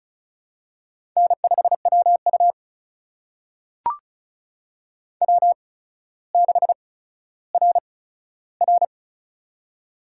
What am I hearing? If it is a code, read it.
Morse code, heard as a keyed tone, about 36 wpm, spaced as for 15 wpm, 700 Hz. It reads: N5WU W6RR
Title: USA Call Signs - 4 Characters - 8x Character Spacing - 35wpm